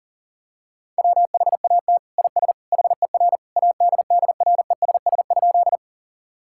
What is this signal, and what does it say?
Morse code, heard as a keyed tone, about 40 wpm, 700 Hz. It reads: WHAT IS HER ADDRESS?
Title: Sentences from Top 1000 Words - Review Part 6 - 40wpm